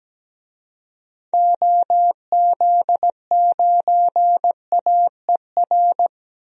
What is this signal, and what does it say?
Morse code, heard as a keyed tone, about 17 wpm, 700 Hz. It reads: OZ9AER
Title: DXCC-100 Call Signs - 6 Characters - Repeated 2x - 17wpm